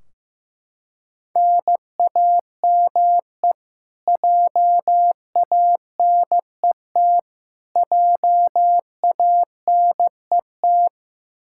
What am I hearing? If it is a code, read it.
Morse code, heard as a keyed tone, about 15 wpm, 700 Hz. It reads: NAME JANET JANET
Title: Practice QSO - Exchange 1 - Beginning - 15wpm